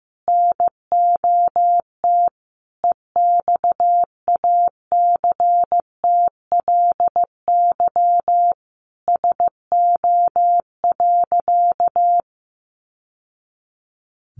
Morse code, heard as a keyed tone, about 15 wpm, 700 Hz. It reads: NOT EXACTLY SO.
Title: Sentences from Top 900 Words 15wpm